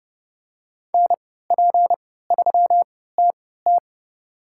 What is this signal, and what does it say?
Morse code, heard as a keyed tone, about 29 wpm, spaced as for 17 wpm, 700 Hz. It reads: NP3TT
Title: USA Call Signs - 5 Characters - 3x Character Spacing - 30wpm